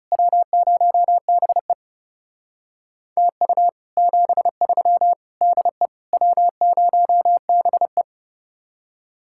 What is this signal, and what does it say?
Morse code, heard as a keyed tone, about 34 wpm, 700 Hz. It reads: W0BE TU 73 DE W0BE
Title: Speed-Racing Common QSO Phrases Encore - 30wpm